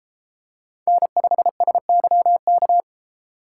Morse code, heard as a keyed tone, about 32 wpm, 700 Hz. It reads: N5SYK